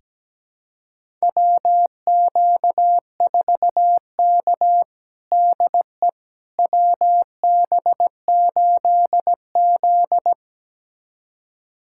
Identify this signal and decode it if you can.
Morse code, heard as a keyed tone, about 17 wpm, 700 Hz. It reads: WQ4K DE WB8Z
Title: USA Callsign DE Callsign - 4 Characters - 17wpm